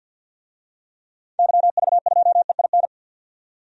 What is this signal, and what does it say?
Morse code, heard as a keyed tone, about 47 wpm, 700 Hz. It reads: YV2EIN
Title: DXCC-100 Call Signs - 6 Characters - 50wpm